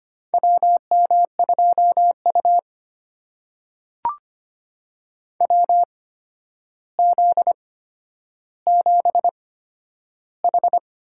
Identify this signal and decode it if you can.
Morse code, heard as a keyed tone, about 25 wpm, spaced as for 11 wpm, 700 Hz. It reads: WM2U WZ7H
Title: USA Call Signs - 4 Characters - 8x Character Spacing - 25wpm